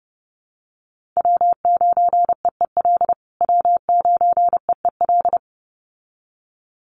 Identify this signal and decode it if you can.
Morse code, heard as a keyed tone, about 30 wpm, 700 Hz. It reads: W9EEL W9EEL
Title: USA Call Signs Repeated 2x 30wpm